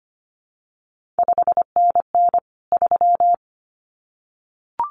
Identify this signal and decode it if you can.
Morse code, heard as a keyed tone, about 25 wpm, 700 Hz. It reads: 5NN 3
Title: CQ WW DX Contest Exchange Encore - 25wpm